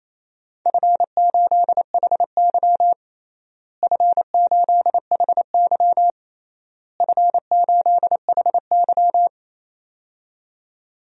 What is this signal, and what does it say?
Morse code, heard as a keyed tone, about 28 wpm, 700 Hz. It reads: F8HY F8HY F8HY
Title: DXCC-100 Call Signs - 4 Characters - Repeated 3x - 28wpm